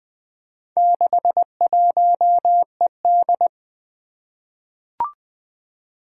Morse code, heard as a keyed tone, about 20 wpm, 700 Hz. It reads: 61ED
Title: Four Challenging Letter-Number Combinations 20wpm